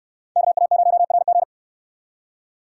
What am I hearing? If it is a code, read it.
Morse code, heard as a keyed tone, about 68 wpm, 700 Hz. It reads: KA0RG